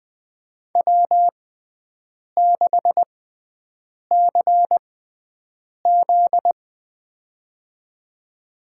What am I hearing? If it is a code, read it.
Morse code, heard as a keyed tone, about 20 wpm, spaced as for 7 wpm, 700 Hz. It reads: W6CZ
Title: USA Call Signs - 6x Character Spacing - 20wpm